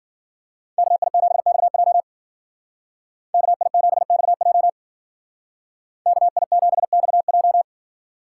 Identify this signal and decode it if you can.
Morse code, heard as a keyed tone, about 50 wpm, 700 Hz. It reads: KI7XJ KI7XJ KI7XJ
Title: Speed Racing - USA 2x2 Call Signs 40wpm